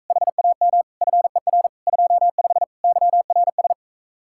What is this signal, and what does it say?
Morse code, heard as a keyed tone, about 44 wpm, 700 Hz. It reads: HAM FER 25 YRS